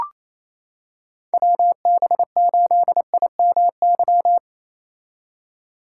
Morse code, heard as a keyed tone, about 28 wpm, 700 Hz. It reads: WB8IMY
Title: K1USN SST QSO Practice - 28wpm